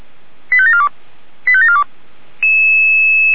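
Five-tone selective call (ZVEI), heard as quick runs of five tones, tones of about 70 ms. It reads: 85742 85742
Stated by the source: Five-tone ZVEI selective call received on an SDR.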